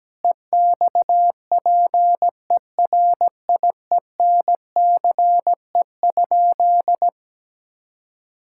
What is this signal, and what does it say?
Morse code, heard as a keyed tone, about 17 wpm, 700 Hz. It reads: EXPERIENCE?
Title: Sentences from Top 1000 Words - Review Part 5 - 17wpm